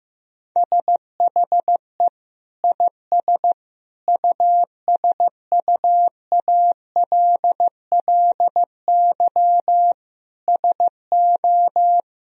Morse code, heard as a keyed tone, about 16 wpm, 700 Hz. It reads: SHE IS USUALLY SO
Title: Sentences from Top 1000 Words - Review Part 3 - 15wpm